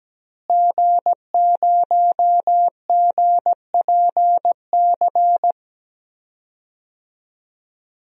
Morse code, heard as a keyed tone, about 17 wpm, 700 Hz. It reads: G0GPC
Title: DXCC-100 Call Signs - 5 Characters - Repeated 3x - 17wpm